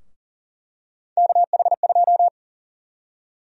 Morse code, heard as a keyed tone, about 40 wpm, 700 Hz. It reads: KH2
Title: USA Call Sign Prefixes - Repeated 2x - 40wpm